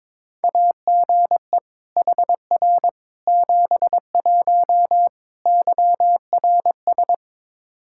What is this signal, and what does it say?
Morse code, heard as a keyed tone, about 22 wpm, 700 Hz. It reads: AGE HR 71 YRS
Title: Practice QSO - Exchange 3 - More Info - 22wpm